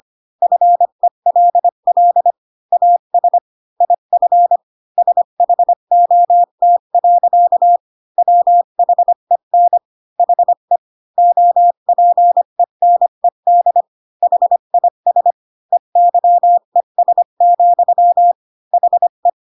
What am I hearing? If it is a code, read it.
Morse code, heard as a keyed tone, about 25 wpm, 700 Hz. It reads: FELL AS IF SHOT. WHEN HE OPENED HIS EYES, HE
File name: Pinocchio / Pinocchio ch1 20 25 ik1hge